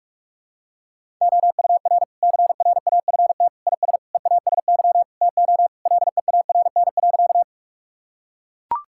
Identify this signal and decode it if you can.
Morse code, heard as a keyed tone, about 42 wpm, 700 Hz. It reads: OUR CRAFT IS EASY TO LEARN.
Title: Sentences from Top 1000 Words - Review Part 6 - 45wpm